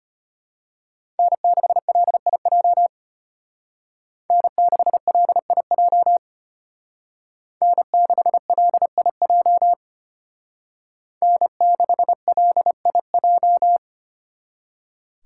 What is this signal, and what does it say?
Morse code, heard as a keyed tone, about 33 wpm, 700 Hz. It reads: N6LIJ N6LIJ N6LIJ N6LIJ